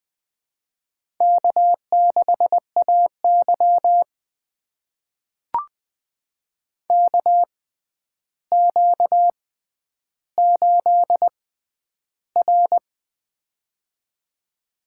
Morse code, heard as a keyed tone, about 20 wpm, spaced as for 10 wpm, 700 Hz. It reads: K6AY KQ8R
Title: USA Call Signs - 4 Characters - 6x Character Spacing - 20wpm